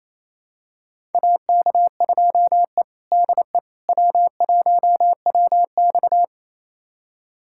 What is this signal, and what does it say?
Morse code, heard as a keyed tone, about 27 wpm, 700 Hz. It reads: AK2E DE W1WX
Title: USA Callsign DE Callsign - 4 Characters - 28wpm